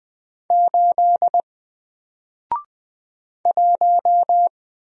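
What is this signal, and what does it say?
Morse code, heard as a keyed tone, about 20 wpm, 700 Hz. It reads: 8 1
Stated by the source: Single letter and number Morse drill at 20 wpm, from N4LCD.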